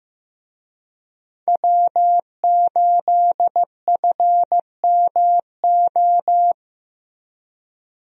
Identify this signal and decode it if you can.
Morse code, heard as a keyed tone, about 15 wpm, 700 Hz. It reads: W8FMO
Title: USA Call Signs - 5 Characters - Repeated 2x - 15wpm